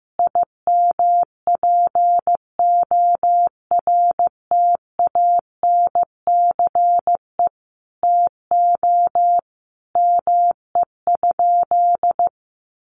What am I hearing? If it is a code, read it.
Morse code, heard as a keyed tone, about 15 wpm, 700 Hz. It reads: IMPORTANCE TO ME?
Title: Sentences from Top 1200 Words 15wpm